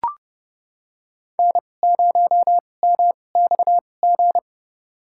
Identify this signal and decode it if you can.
Morse code, heard as a keyed tone, about 31 wpm, spaced as for 22 wpm, 700 Hz. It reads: N0MXG
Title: USA Call Signs - 5 Characters - 2x Character Spacing - 30wpm